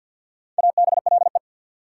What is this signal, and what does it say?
Morse code, heard as a keyed tone, about 66 wpm, spaced as for 58 wpm, 690 Hz. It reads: ABLE
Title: Speed-Racing Prefixes and Suffixes 50wpm